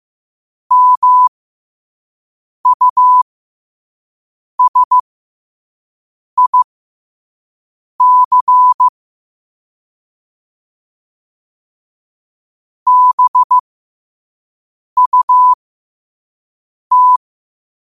Morse code, heard as a keyed tone, about 15 wpm, spaced as for 5 wpm, 1000 Hz. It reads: MUSIC BUT
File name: qotdfiles / 5wpm